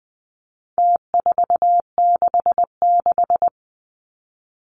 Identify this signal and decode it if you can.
Morse code, heard as a keyed tone, about 20 wpm, 700 Hz. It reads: T466
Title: Numbers - Contest Style - 4-Digit 20wpm